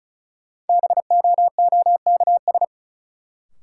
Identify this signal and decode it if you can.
Morse code, heard as a keyed tone, about 36 wpm, 700 Hz. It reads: BOOKS
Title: Sentences from Top 1200 Words - New Word Drill Alt - 35wpm